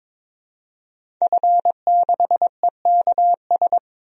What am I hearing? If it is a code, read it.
Morse code, heard as a keyed tone, about 22 wpm, 700 Hz. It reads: F6EKS
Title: DXCC-100 Call Signs - 5 Characters - 22wpm